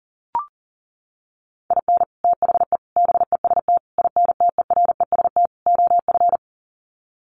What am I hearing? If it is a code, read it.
Morse code, heard as a keyed tone, about 40 wpm, 700 Hz. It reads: IN THE BEST INTEREST OF
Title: Sets of 5 Words Encore 40wpm